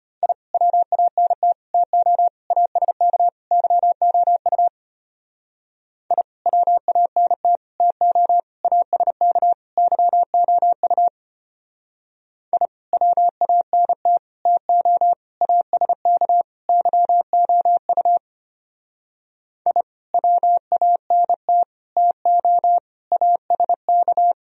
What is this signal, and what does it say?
Morse code, heard as a keyed tone, about 32 wpm, 700 Hz. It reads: I WANT TO ASK YOU I WANT TO ASK YOU I WANT TO ASK YOU I WANT TO ASK